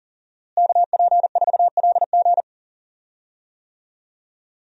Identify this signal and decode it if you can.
Morse code, heard as a keyed tone, about 40 wpm, 700 Hz. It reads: KP4LG
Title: DXCC-100 Call Signs - 5 Characters - Repeated 2x - 40wpm